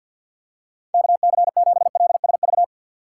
Morse code, heard as a keyed tone, about 52 wpm, 700 Hz. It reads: KC7LSV
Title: USA Call Signs - 6 Characters - 3x Character Spacing - 50wpm